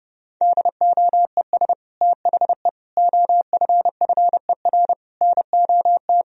Morse code, heard as a keyed tone, about 30 wpm, 700 Hz. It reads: DOES THE OFFER NOT
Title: Sentences from Top 1000 Words - Review Part 4 - 30wpm